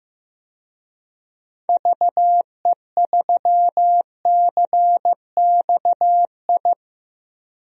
Morse code, heard as a keyed tone, about 15 wpm, 700 Hz. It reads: VE3CXI
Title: DXCC-100 Call Signs - 6 Characters - Repeated 3x - 15wpm